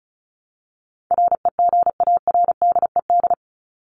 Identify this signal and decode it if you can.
Morse code, heard as a keyed tone, about 34 wpm, 700 Hz. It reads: REGARDED
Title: ED Suffix - Top 100 Words with Base Words 35wpm